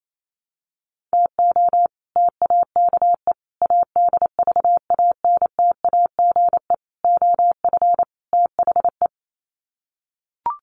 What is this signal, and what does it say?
Morse code, heard as a keyed tone, about 28 wpm, 700 Hz. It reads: TO TAKE ADVANTAGE OF THE
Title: Sets of 5 Words 28wpm